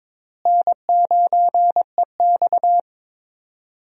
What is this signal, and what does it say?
Morse code, heard as a keyed tone, about 22 wpm, 700 Hz. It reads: N9EX
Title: USA Call Signs - 4 Characters - Repeated 3x - 22wpm